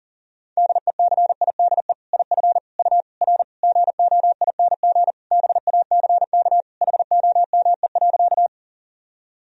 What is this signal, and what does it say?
Morse code, heard as a keyed tone, about 37 wpm, 700 Hz. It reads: DECIDE IF U R GOING BACK HOME.
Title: Sentences from Top 500 Words and Abbreviations 40wpm